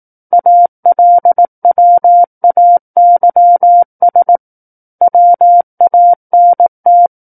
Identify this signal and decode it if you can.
Morse code, heard as a keyed tone, about 18 wpm, 700 Hz. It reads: ALWAYS WANT
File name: 18 WPM